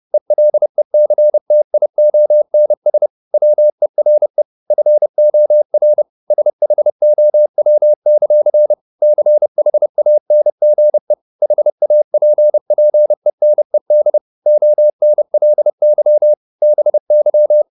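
Morse code, heard as a keyed tone, about 30 wpm, 600 Hz. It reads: ELECTIONS WERE FOR SHOW; CHANGE HAPPENED ONLY BY
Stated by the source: Morse code at 30 wpm, the novel Never by Ken Follett in English, from CW med Gnister.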